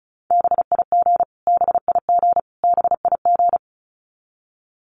Morse code, heard as a keyed tone, about 34 wpm, 700 Hz. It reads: BIG BIG BIG
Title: Top 200 Words Repeated 35wpm